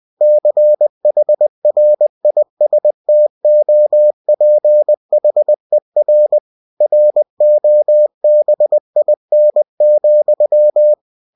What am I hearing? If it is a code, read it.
Morse code, heard as a keyed tone, about 20 wpm, 600 Hz. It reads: CHRISTOPHER ROBIN,
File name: cw-books / winnie-the-pooh / winnie-20wpm-page-0029